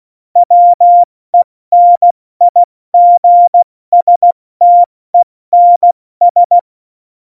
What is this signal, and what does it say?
Morse code, heard as a keyed tone, about 16 wpm, spaced as for 14 wpm, 700 Hz. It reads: WENIGSTENS
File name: Morsekurs / Witze / Witz 13. Oktober 2020, 70 F80